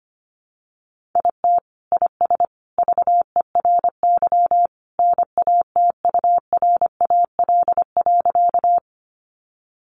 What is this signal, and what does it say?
Morse code, heard as a keyed tone, about 25 wpm, 700 Hz. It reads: IT IS VERY NATURAL.
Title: Sentences from Top 700 Words 25wpm